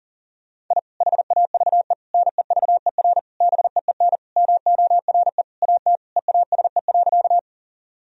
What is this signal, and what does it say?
Morse code, heard as a keyed tone, about 40 wpm, 700 Hz. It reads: I HAVE NEVER BEEN MORE AT EASE.
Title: Sentences from Top 1000 Words - Review Part 1 - 40wpm